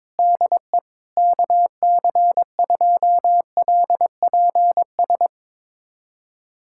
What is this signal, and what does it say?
Morse code, heard as a keyed tone, about 22 wpm, 700 Hz. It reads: DE KC2LPS
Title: USA Callsign DE Callsign 22wpm